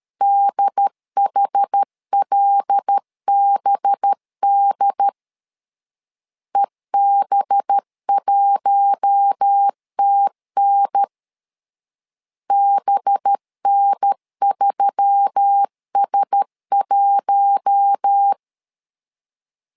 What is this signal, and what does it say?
Morse code, heard as a keyed tone, about 13 wpm, 800 Hz. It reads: DHLBD EB1TN BN3S1
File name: LEZIONI / 5 LBVC / 5 LB 12wpm 1906